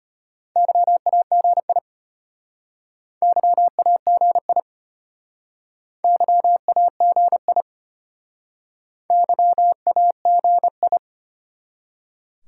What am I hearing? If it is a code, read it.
Morse code, heard as a keyed tone, about 31 wpm, 700 Hz. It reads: YAGI YAGI YAGI YAGI
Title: Speed-Racing Top 180 QSO Elements 25wpm